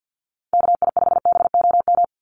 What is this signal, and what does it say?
Morse code, heard as a keyed tone, about 50 wpm, 700 Hz. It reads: KI5BOM
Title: USA Call Signs 50wpm